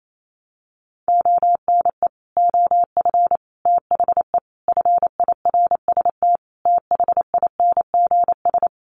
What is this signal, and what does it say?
Morse code, heard as a keyed tone, about 28 wpm, 700 Hz. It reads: ONE OF THE FIRST THINGS
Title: Sets of 5 Words Encore 28wpm